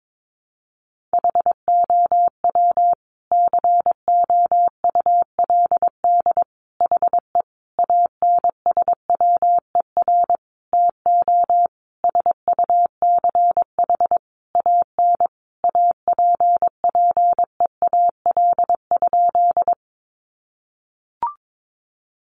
Morse code, heard as a keyed tone, about 22 wpm, 700 Hz. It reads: HOW COULD HE ANSWER TO SUCH AN APPEAL?
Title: Sentences from Top 1300 Words 22wpm